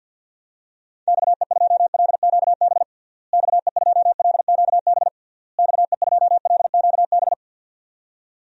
Morse code, heard as a keyed tone, about 52 wpm, 700 Hz. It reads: XE2LQB XE2LQB XE2LQB
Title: DXCC-100 Call Signs - 6 Characters - Repeated 3x - 50wpm